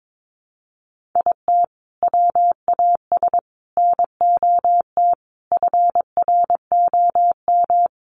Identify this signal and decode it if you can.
Morse code, heard as a keyed tone, about 22 wpm, 700 Hz. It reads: IT WAS NOT FROM
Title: Sentences from Top 700 Words 22wpm